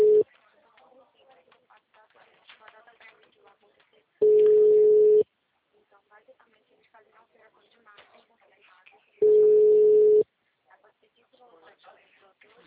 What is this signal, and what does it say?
Ringback tone (Europe, 425 Hz), with speech.